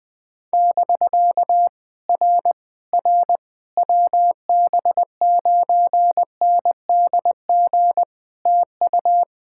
Morse code, heard as a keyed tone, about 20 wpm, 700 Hz. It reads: <BK> R R WB9NDG TU
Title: POTA QSO Practice - 20wpm